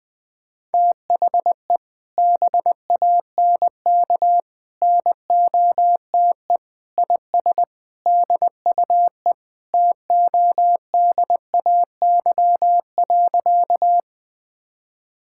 Morse code, heard as a keyed tone, about 20 wpm, 700 Hz. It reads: THE BANK NOTE IS DUE TODAY.
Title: Sentences from Top 1000 Words - Review Part 6 - 20wpm